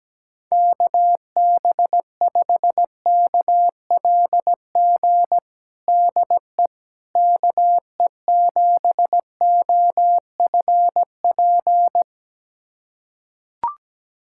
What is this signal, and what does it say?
Morse code, heard as a keyed tone, about 17 wpm, 700 Hz. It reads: KB5KLG DE KE7OFP
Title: USA Callsign DE Callsign 17wpm